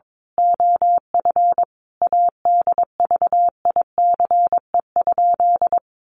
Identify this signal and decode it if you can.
Morse code, heard as a keyed tone, about 22 wpm, 700 Hz. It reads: OF ADVICE?
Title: Sentences from Top 1100 Words 22wpm